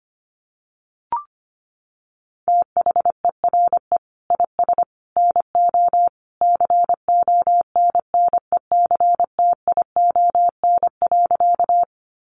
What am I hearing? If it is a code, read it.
Morse code, heard as a keyed tone, about 25 wpm, 700 Hz. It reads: THERE IS NO CONNECTION.
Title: Sentences from Top 1300 Words 25wpm